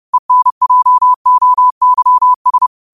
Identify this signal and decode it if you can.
Morse code, heard as a keyed tone, about 30 wpm, 1000 Hz. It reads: ENJOYS